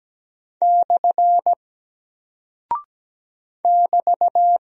Morse code, heard as a keyed tone, about 17 wpm, 700 Hz. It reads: / <BT>
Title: Single Character (Letter:Number:Punctuation:Prosign) 17wpm